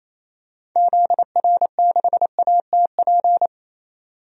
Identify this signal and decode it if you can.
Morse code, heard as a keyed tone, about 28 wpm, 700 Hz. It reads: ZR6ATP
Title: DXCC-100 Call Signs - 6 Characters - 28wpm